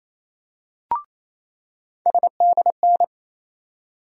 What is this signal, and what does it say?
Morse code, heard as a keyed tone, about 28 wpm, 700 Hz. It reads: SDN
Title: USA Call Sign Postfixes - 3 Letters 28wpm